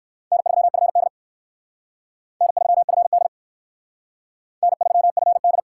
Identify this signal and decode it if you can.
Morse code, heard as a keyed tone, about 57 wpm, 700 Hz. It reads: N3FD N3FD N3FD